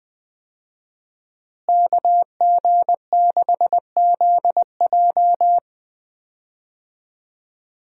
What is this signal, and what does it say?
Morse code, heard as a keyed tone, about 20 wpm, 700 Hz. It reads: KG6ZJ
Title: USA Call Signs - 5 Characters - Repeated 3x - 20wpm